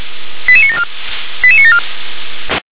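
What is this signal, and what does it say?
Five-tone selective call (ZVEI), heard as quick runs of five tones, tones of about 70 ms. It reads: 88974 88974
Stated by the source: Signal received via SDR, ZVEI five-tone.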